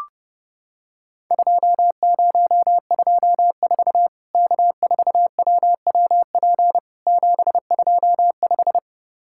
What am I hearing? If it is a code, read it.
Morse code, heard as a keyed tone, about 30 wpm, 700 Hz. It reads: 2024 K4WWP 725
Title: Worked All Europe DX Contest - QTC Exchange - 30wpm